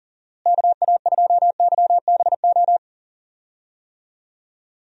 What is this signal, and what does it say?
Morse code, heard as a keyed tone, about 40 wpm, 700 Hz. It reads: KA2YBO
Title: USA Call Signs - 6 Characters - Repeated 2x - 40wpm